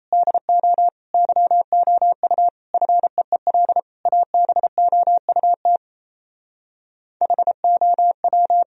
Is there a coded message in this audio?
Yes: Morse code from 0.1 to 8.6 s, heard as a keyed tone, about 38 wpm, 700 Hz. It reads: DO YOU FEEL ABOUT HOW